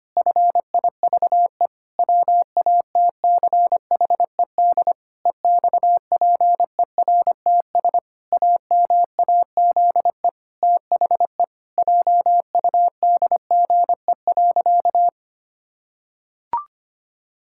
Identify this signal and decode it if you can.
Morse code, heard as a keyed tone, about 25 wpm, 700 Hz. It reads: FIVE WATCHED EXPERTS AMAZE THE JUDGE.
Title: Pangram Challenge v2 - 25wpm